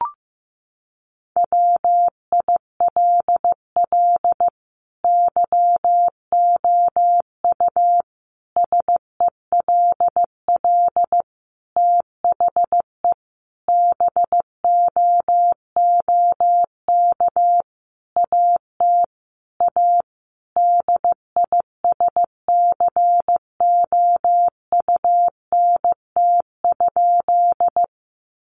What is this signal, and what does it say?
Morse code, heard as a keyed tone, about 15 wpm, 700 Hz. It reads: WILL YOU SELL THE BOOK AT A DISCOUNT?
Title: Sentences from Top 1200 Words 15wpm